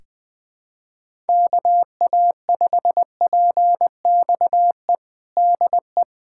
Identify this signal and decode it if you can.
Morse code, heard as a keyed tone, about 20 wpm, 700 Hz. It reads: KA5PXE DE
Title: USA Callsign DE Callsign 20wpm